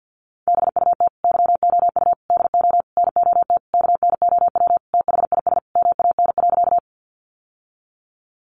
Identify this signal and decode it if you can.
Morse code, heard as a keyed tone, about 50 wpm, 700 Hz. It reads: BUT YOU DO NOT KNOW THIS MAN.
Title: Sentences from Top 600 Words 50wpm